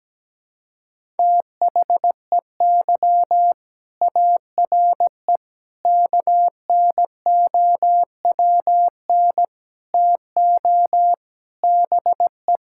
Morse code, heard as a keyed tone, about 17 wpm, 700 Hz. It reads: THEY ARE KNOWN TO BE
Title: Sentences from Top 1000 Words - Review Part 5 - 17wpm